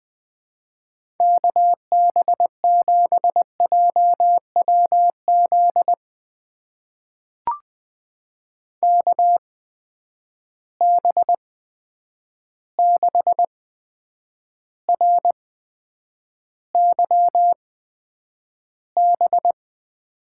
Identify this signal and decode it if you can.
Morse code, heard as a keyed tone, about 21 wpm, spaced as for 9 wpm, 700 Hz. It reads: KB7JWZ KB6RYB
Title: USA Call Signs - 6 Characters - 8x Character Spacing - 20wpm